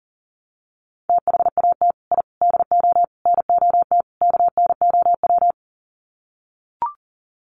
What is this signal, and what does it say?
Morse code, heard as a keyed tone, about 37 wpm, 700 Hz. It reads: THAT I DO NOT KNOW
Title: Sets of 5 Words Encore2 40wpm